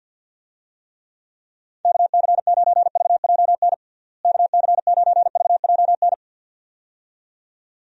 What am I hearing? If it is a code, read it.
Morse code, heard as a keyed tone, about 49 wpm, 700 Hz. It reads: KC9VJN KC9VJN
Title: USA Call Signs - 6 Characters - Repeated 2x - 50wpm